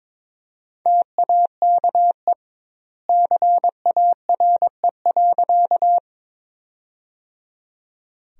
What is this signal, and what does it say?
Morse code, heard as a keyed tone, about 22 wpm, 700 Hz. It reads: TAKE CARE.
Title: 2x Word-Spacing - Sentences from Top 300 Words - 22wpm